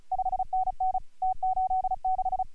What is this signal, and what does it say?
Morse code, heard as a keyed tone, about 36 wpm, 750 Hz. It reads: 5NN T86
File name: AUDIO2018 / HG1G CW